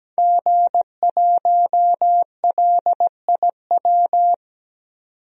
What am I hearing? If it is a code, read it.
Morse code, heard as a keyed tone, about 17 wpm, 700 Hz. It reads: G1LIW
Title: DXCC-100 Call Signs - 5 Characters - 17wpm